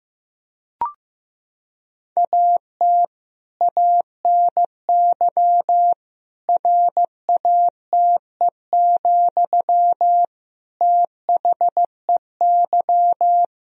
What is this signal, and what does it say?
Morse code, heard as a keyed tone, about 15 wpm, 700 Hz. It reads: AT ANY RATE, THEY
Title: Sentences from Top 1000 Words - Review Part 4 - 15wpm